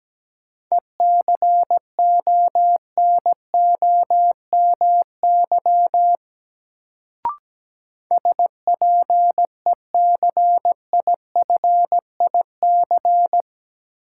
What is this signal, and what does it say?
Morse code, heard as a keyed tone, about 17 wpm, 700 Hz. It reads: ECONOMY SPECIFIC
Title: Sentences from Top 1000 Words - Review Part 3 - New Word Drill - 17wpm